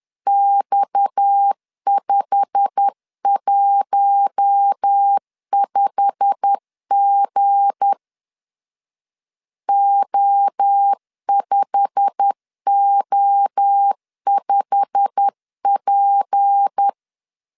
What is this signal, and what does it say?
Morse code, heard as a keyed tone, about 10 wpm, 800 Hz. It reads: X515G O5O5P
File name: LEZIONI / 8 YJ45 / 8 YJ45 10wpm 2549